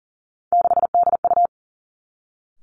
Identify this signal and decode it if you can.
Morse code, heard as a keyed tone, about 40 wpm, 700 Hz. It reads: 6DU
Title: Three Challenging Letter-Number Combinations 40wpm